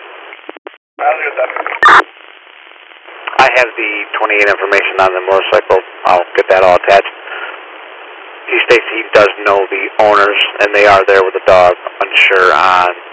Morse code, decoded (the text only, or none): none